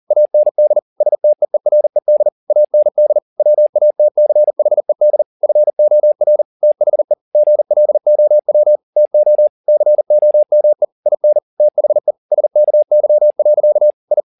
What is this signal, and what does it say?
Morse code, heard as a keyed tone, about 40 wpm, 600 Hz. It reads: AND STEERED AND WATCHED FOR THE GLOW TO COME IN THE SKY. I
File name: the old man and the sea 2 40wpm